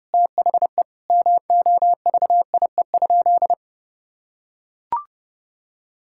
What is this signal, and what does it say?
Morse code, heard as a keyed tone, about 30 wpm, 700 Hz. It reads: THE MOVIE?
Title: Sentences from Top 1000 Words - Review Part 5 - 30wpm